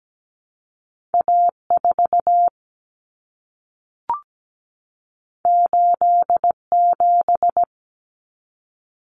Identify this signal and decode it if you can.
Morse code, heard as a keyed tone, about 17 wpm, 700 Hz. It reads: A4 87
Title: Common Cut Numbers - 2-Digit 17wpm